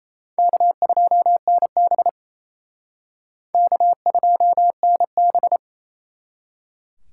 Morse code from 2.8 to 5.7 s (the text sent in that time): K2NB